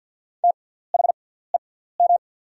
Morse code, heard as a keyed tone, about 48 wpm, spaced as for 17 wpm, 700 Hz. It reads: THEM
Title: 6x Character Spacing - Top 200 Words - 50wpm